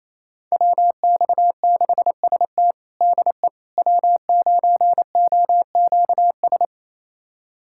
Morse code, heard as a keyed tone, about 28 wpm, 700 Hz. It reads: WX6ST DE W9OQS
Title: USA Callsign DE Callsign - 5 Characters - 28wpm